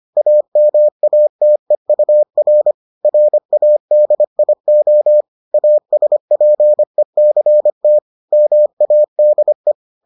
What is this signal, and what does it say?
Morse code, heard as a keyed tone, about 25 wpm, 600 Hz. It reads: AMATEUR RADIO ASPECT MADE